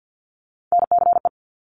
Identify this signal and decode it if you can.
Morse code, heard as a keyed tone, about 50 wpm, 700 Hz. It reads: NCE
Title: Top Three Letter Combinations in Words 50wpm